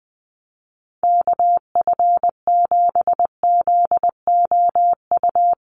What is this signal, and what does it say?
Morse code, heard as a keyed tone, about 20 wpm, 700 Hz. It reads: KF7ZOU
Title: USA Call Signs Repeated 3x 20wpm